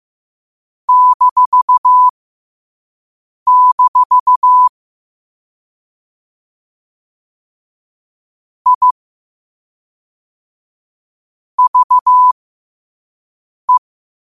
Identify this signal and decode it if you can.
Morse code, heard as a keyed tone, about 15 wpm, spaced as for 5 wpm, 1000 Hz. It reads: -- I VE
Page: Quotes for Wed, 13 Aug 2025 in Morse Code at 5 words per minute.